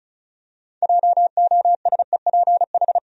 Morse code, heard as a keyed tone, about 36 wpm, 700 Hz. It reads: JOSEPH
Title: USA Common Names - 35wpm